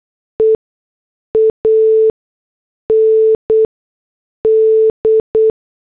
Morse code, heard as a keyed tone, about 8 wpm, spaced as for 6 wpm, 440 Hz. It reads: E AND